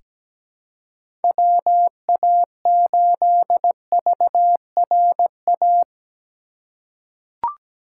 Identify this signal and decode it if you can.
Morse code, heard as a keyed tone, about 17 wpm, 700 Hz. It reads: WA8VRA
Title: USA Call Signs - 6 Characters - 2x Character Spacing - 17wpm